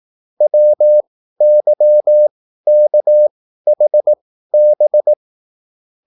Morse code, heard as a keyed tone, about 18 wpm, spaced as for 13 wpm, 600 Hz. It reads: WYKHB